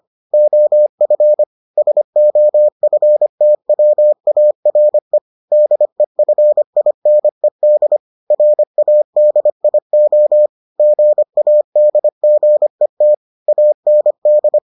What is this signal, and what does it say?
Morse code, heard as a keyed tone, about 25 wpm, 600 Hz. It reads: OF SOFTWARE DEFINED RADIO GADGET AND